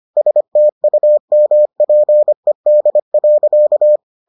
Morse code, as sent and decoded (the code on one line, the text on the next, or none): ... - ..- -- .--. -.. .-.-.-
STUMPED.